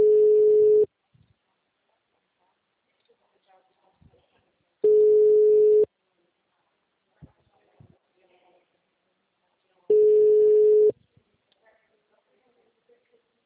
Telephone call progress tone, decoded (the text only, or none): ringback tone